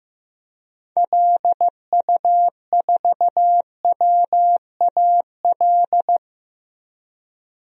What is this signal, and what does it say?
Morse code, heard as a keyed tone, about 15 wpm, 700 Hz. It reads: LU4WAL
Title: DXCC-100 Call Signs - 6 Characters - Repeated 3x - 15wpm